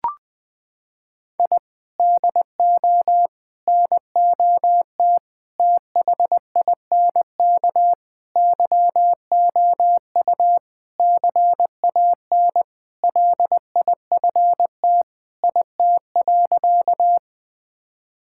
Morse code, heard as a keyed tone, about 21 wpm, 700 Hz. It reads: I DO NOT THINK YOU CAN LIFT IT.
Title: Sentences with 5-Word Phrases - 20wpm